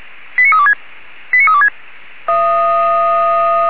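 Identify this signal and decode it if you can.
Five-tone selective call (ZVEI), heard as quick runs of five tones, tones of about 70 ms. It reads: 89326 89326
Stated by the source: ZVEI five-tone selective call received on an SDR.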